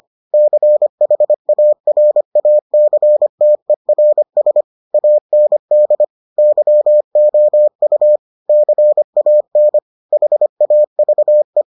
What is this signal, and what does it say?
Morse code, heard as a keyed tone, about 25 wpm, 600 Hz. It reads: CHARACTERS AND YOU CAN HAVE